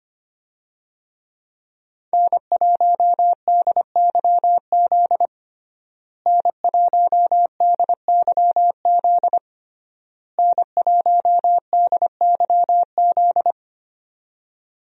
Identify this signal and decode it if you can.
Morse code, heard as a keyed tone, about 25 wpm, 700 Hz. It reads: N1DYZ N1DYZ N1DYZ
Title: USA Call Signs - 5 Characters - Repeated 3x - 25wpm